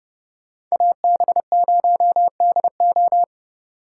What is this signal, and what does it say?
Morse code, heard as a keyed tone, about 36 wpm, 700 Hz. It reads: AB0DO
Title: Speed Racing - USA 2x2 Call Signs 30wpm